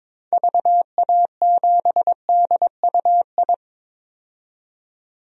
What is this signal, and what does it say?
Morse code, heard as a keyed tone, about 22 wpm, 700 Hz. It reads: VA7DUI
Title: DXCC-100 Call Signs - 6 Characters - 22wpm